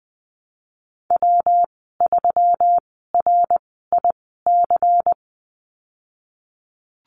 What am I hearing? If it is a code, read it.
Morse code, heard as a keyed tone, about 19 wpm, spaced as for 14 wpm, 700 Hz. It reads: W3RIC
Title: 128 Lesson 21g - Number 6 - Call Signs 20wpm